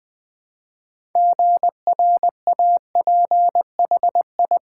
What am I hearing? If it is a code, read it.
Morse code, heard as a keyed tone, about 20 wpm, 700 Hz. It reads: GRAPHI
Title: Build A Word - 3 Letters At A Time - 8-Letter Words - 20wpm